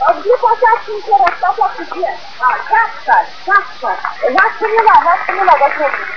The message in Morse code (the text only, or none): none